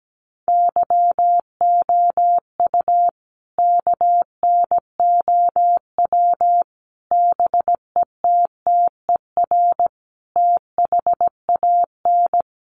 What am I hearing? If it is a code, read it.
Morse code, heard as a keyed tone, about 17 wpm, 700 Hz. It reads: YOU KNOW BETTER THAN
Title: Sentences from Top 400 Words 17wpm